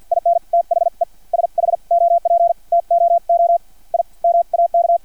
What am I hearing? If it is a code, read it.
Morse code, heard as a keyed tone, about 50 wpm, spaced as for 42 wpm, 680 Hz. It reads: IN THE SHOP TOO I MAK
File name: archived / CRSnet